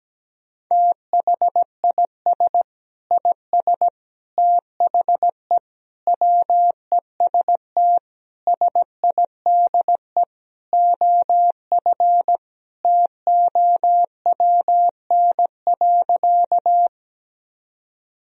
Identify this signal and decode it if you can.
Morse code, heard as a keyed tone, about 17 wpm, 700 Hz. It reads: THIS IS THE WEST SIDE OF TOWN.
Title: Sentences from Top 1000 Words - Review Part 2 - 17wpm